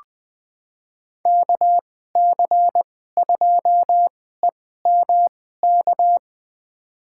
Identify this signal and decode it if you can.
Morse code, heard as a keyed tone, about 20 wpm, spaced as for 15 wpm, 700 Hz. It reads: KC2EMK
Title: USA Call Signs - 6 Characters - 2x Character Spacing - 20wpm